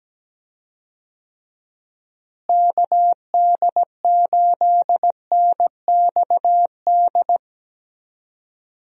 Morse code, heard as a keyed tone, about 17 wpm, 700 Hz. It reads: KD8NXD
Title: USA Call Signs - 6 Characters - Repeated 3x - 17wpm